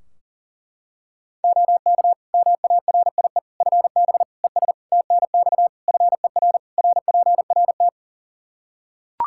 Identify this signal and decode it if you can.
Morse code, heard as a keyed tone, about 40 wpm, 700 Hz. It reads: OK MARIE FB ES TNX FER RPRT
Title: Practice QSO - Exchange 2 - Info - 40wpm